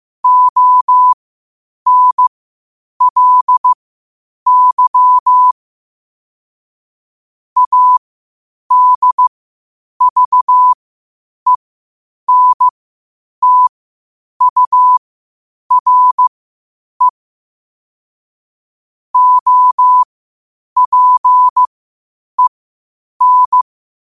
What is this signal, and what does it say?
Morse code, heard as a keyed tone, about 15 wpm, spaced as for 8 wpm, 1000 Hz. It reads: ONLY ADVENTURE OPEN
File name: qotdfiles / 8wpm